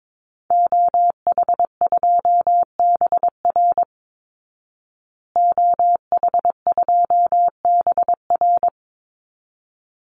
Morse code, heard as a keyed tone, about 22 wpm, 700 Hz. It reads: OH2BR OH2BR
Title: Call Signs Repeated 3x - 22wpm